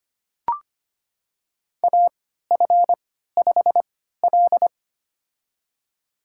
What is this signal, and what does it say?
Morse code, heard as a keyed tone, about 25 wpm, spaced as for 14 wpm, 700 Hz. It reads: AF5L
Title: USA Call Signs - 3x Character Spacing - 25wpm